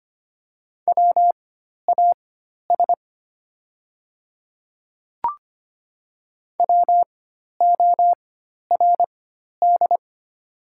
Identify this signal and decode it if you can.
Morse code, heard as a keyed tone, about 25 wpm, spaced as for 12 wpm, 700 Hz. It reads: WAS WORD
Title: 4x Character Spacing - Top 100 Words - 25wpm